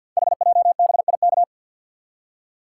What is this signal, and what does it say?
Morse code, heard as a keyed tone, about 53 wpm, 700 Hz. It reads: HJBIK
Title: Code Groups - 5 Letters - 50wpm